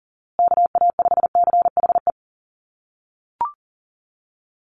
Morse code, heard as a keyed tone, about 40 wpm, 700 Hz. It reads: KA5CHE
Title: USA Call Signs Repeated 3x 40wpm